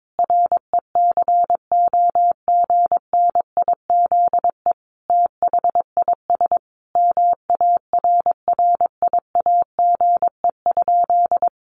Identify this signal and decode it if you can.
Morse code, heard as a keyed tone, about 22 wpm, 700 Hz. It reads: RECOGNIZE THIS MARRIAGE?
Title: Sentences from Top 1400 Words 22wpm